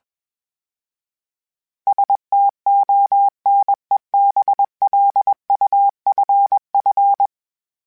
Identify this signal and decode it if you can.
Morse code, heard as a keyed tone, about 21 wpm, 800 Hz. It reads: STONEBLUFF
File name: fromSMs / FromSMs mp3s / Cities@20